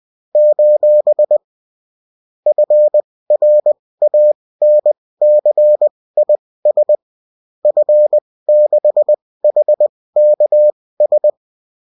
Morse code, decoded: : FRANCIS F6HKS